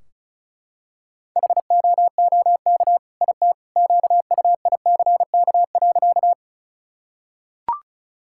Morse code, heard as a keyed tone, about 34 wpm, 700 Hz. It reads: HOOK IT QUICK.
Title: Sentences from Top 1600 Words 35wpm